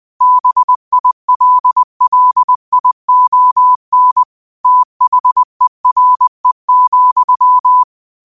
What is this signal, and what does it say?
Morse code, heard as a keyed tone, about 20 wpm, 1000 Hz. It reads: BILLION THERE,
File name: qotdfiles / 20wpm